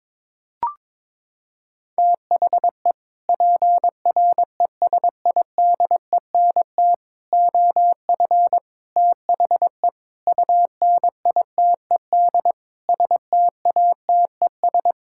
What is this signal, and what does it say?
Morse code, heard as a keyed tone, about 23 wpm, 700 Hz. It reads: THE PRESIDENT OF THE UNITED STATES
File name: practice / 22wpm / Sentences with 4-Word Phrases - 22wpm